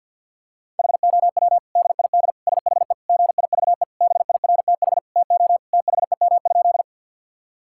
Morse code, heard as a keyed tone, about 50 wpm, 700 Hz. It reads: HOW DID SHE GIVE BIRTH TO THEM?
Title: Sentences from Top 1000 Words - Review Part 5 - 50wpm